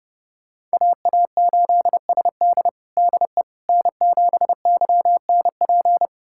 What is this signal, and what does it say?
Morse code, heard as a keyed tone, about 29 wpm, 700 Hz. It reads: AA8SD DE N7YNP
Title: USA Callsign DE Callsign - 5 Characters - 30wpm